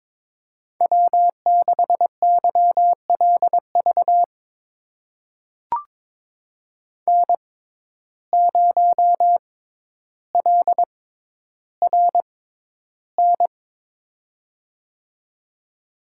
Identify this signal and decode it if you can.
Morse code, heard as a keyed tone, about 22 wpm, spaced as for 11 wpm, 700 Hz. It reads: W6YLV N0LRN
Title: USA Call Signs - 5 Characters - 6x Character Spacing - 22wpm